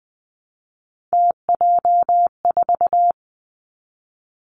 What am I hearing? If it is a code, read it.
Morse code, heard as a keyed tone, about 20 wpm, 700 Hz. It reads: TJ4
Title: Three Challenging Letter-Number Combinations 20wpm